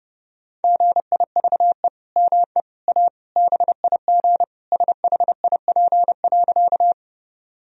Morse code, heard as a keyed tone, about 32 wpm, 700 Hz. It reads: GIVE ME A BIG SHIP.
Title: Sentences from Top 1000 Words - Review Part 2 - 30wpm